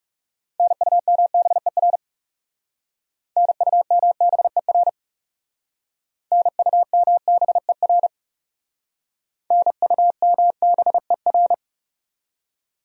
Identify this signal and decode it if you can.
Morse code, heard as a keyed tone, about 39 wpm, 700 Hz. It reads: NUMBER NUMBER NUMBER NUMBER